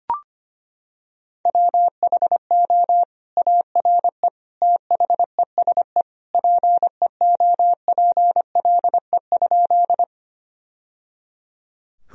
Morse code, heard as a keyed tone, about 25 wpm, 700 Hz. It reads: WHO ARE THESE PEOPLE?
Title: Sentences from Top 1000 Words - Review Part 1 - 25wpm